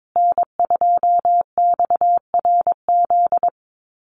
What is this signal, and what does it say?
Morse code, heard as a keyed tone, about 22 wpm, 700 Hz. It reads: N2XRZ
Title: USA Call Signs Repeated 3x 22wpm